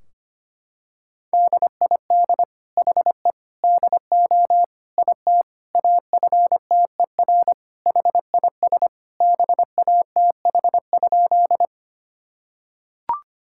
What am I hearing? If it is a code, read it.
Morse code, heard as a keyed tone, about 25 wpm, 700 Hz. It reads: DID HE DO IT AFTER HIS BATH?
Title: Sentences from Top 1600 Words 25wpm